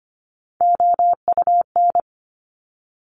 Morse code, heard as a keyed tone, about 25 wpm, 700 Hz. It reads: OUN
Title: Top Three Letter Combinations in Words 25wpm